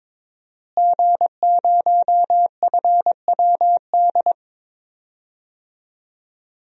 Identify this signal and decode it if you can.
Morse code, heard as a keyed tone, about 22 wpm, 700 Hz. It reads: G0FWD
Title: DXCC-100 Call Signs - 5 Characters - Repeated 2x - 22wpm